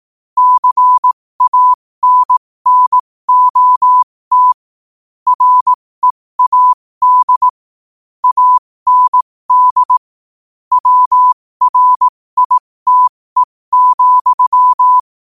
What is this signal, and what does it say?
Morse code, heard as a keyed tone, about 18 wpm, spaced as for 15 wpm, 1000 Hz. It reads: CANNOT READ AND WRITE,
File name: qotdfiles / 15wpm